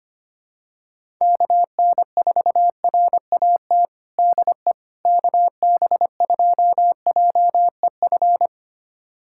Morse code, heard as a keyed tone, about 25 wpm, 700 Hz. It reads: KN4RAT DE KB2JEF
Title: USA Callsign DE Callsign - 6 Characters - 25wpm